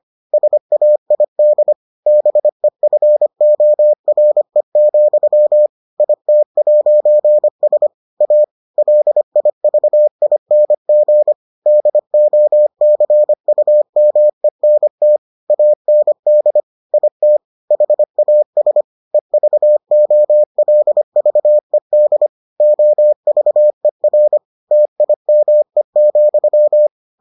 Morse code, heard as a keyed tone, about 25 wpm, 600 Hz. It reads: SAID BEFORE, IT'S A LIVING DOCUMENT AND IT HAS EVOLVED OVER TIME,